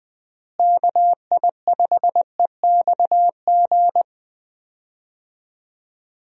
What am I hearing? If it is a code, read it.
Morse code, heard as a keyed tone, about 20 wpm, 700 Hz. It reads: KI5EXG
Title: USA Call Signs - 6 Characters - Repeated 2x - 20wpm